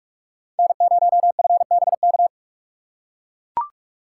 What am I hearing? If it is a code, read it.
Morse code, heard as a keyed tone, about 44 wpm, 700 Hz. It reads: N0FBK
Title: USA Call Signs - 3x Character Spacing - 45wpm